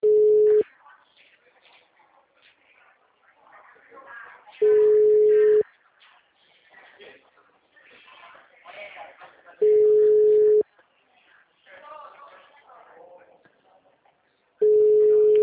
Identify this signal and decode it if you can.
Ringback tone (Europe, 425 Hz), with speech.